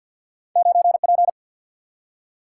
Morse code, heard as a keyed tone, about 47 wpm, 700 Hz. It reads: 9P
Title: Single-Letter Single-Number 50wpm